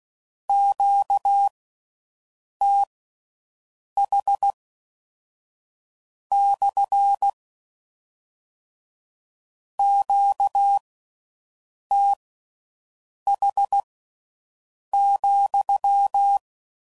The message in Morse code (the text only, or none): QTH / QTH,